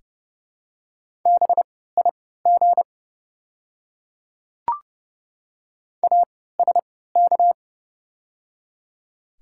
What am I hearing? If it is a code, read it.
Morse code, heard as a keyed tone, about 30 wpm, spaced as for 17 wpm, 700 Hz. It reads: BIG ASK
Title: 3x Character Spacing - Top 200 Words - 30wpm